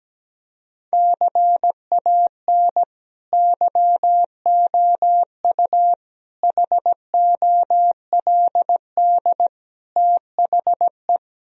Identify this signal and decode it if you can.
Morse code, heard as a keyed tone, about 17 wpm, 700 Hz. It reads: CAN YOU HOLD THE
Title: Sentences from Top 1000 Words - Review Part 2 - 17wpm